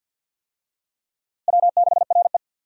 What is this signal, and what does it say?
Morse code, heard as a keyed tone, about 47 wpm, 700 Hz. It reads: W6RE
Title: USA Call Signs - 4 Characters - Repeated 3x - 50wpm